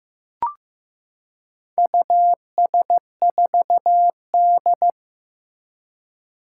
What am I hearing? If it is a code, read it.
Morse code, heard as a keyed tone, about 15 wpm, 700 Hz. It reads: US4D
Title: Four Challenging Letter-Number Combinations 15wpm